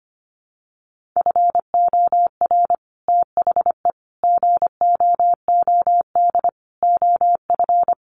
Morse code, heard as a keyed tone, about 25 wpm, 700 Hz. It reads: FOR THE GOOD OF
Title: Sets of 5 Words Encore2 25wpm